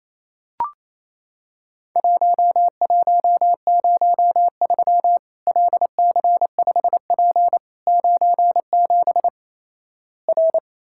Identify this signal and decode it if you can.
Morse code, heard as a keyed tone, about 28 wpm, 700 Hz. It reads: 1103 LC5P 97 R
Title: Worked All Europe DX Contest - QTC Exchange - 28wpm